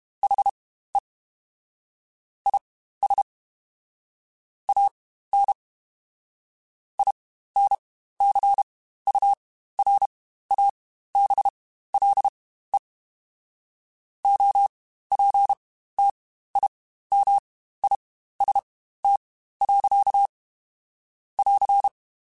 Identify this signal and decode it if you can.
Morse code, heard as a keyed tone, about 32 wpm, spaced as for 15 wpm, 800 Hz. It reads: HE IS AN INCURABLE OPTIMIST. <AR>
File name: Quotes Spaced / Quotes Spaced-mp3s / Optimist sp @30